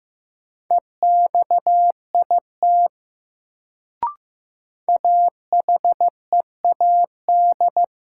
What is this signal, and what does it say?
Morse code, heard as a keyed tone, about 15 wpm, 700 Hz. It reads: EXIT AHEAD
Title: Sentences from Top 1100 Words - New Word Drill Alt - 15wpm